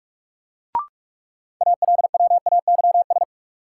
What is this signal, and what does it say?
Morse code, heard as a keyed tone, about 46 wpm, 700 Hz. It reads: ALWAYS
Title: Sentences from Top 300 Words - New Word Drill - 45wpm